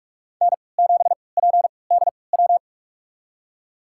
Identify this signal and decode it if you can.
Morse code, heard as a keyed tone, about 41 wpm, spaced as for 25 wpm, 700 Hz. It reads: N7PDW
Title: USA Call Signs - 3x Character Spacing - 45wpm